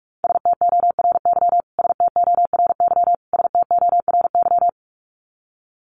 Morse code, heard as a keyed tone, about 44 wpm, 700 Hz. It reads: STORY STORY STORY
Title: Top 300 Words Repeated 45wpm